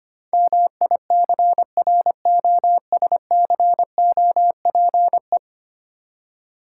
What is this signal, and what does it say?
Morse code, heard as a keyed tone, about 25 wpm, 700 Hz. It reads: MICROSCOPE
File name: Easy Words - Speed-Racing with Hints - 25wpm